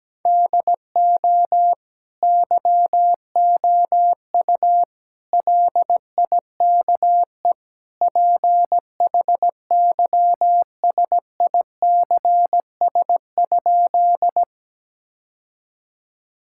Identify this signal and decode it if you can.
Morse code, heard as a keyed tone, about 17 wpm, 700 Hz. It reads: DO YOU LIKE PHYSICS?
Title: Sentences from Top 1000 Words - Review Part 5 - 17wpm